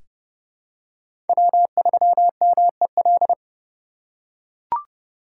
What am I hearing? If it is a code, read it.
Morse code, heard as a keyed tone, about 30 wpm, 700 Hz. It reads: W3MEL
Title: K1USN SST QSO Practice - 30wpm